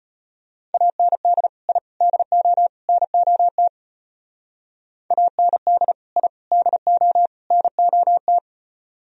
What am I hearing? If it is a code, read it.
Morse code, heard as a keyed tone, about 32 wpm, 700 Hz. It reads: AND I DO NOT AND I DO NOT